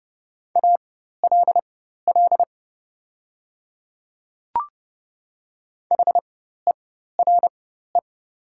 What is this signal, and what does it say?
Morse code, heard as a keyed tone, about 30 wpm, spaced as for 14 wpm, 700 Hz. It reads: ALL HERE
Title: 4x Character Spacing - Top 200 Words - 30wpm